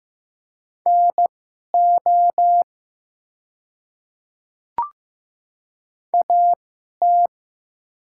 Morse code, heard as a keyed tone, about 15 wpm, spaced as for 11 wpm, 700 Hz. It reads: NO AT
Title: 2x Character Spacing - Top 100 Words - 15wpm